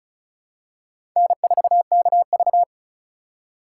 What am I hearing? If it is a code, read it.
Morse code, heard as a keyed tone, about 36 wpm, 700 Hz. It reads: N4KV
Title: USA Call Signs - 4 Characters - 3x Character Spacing - 35wpm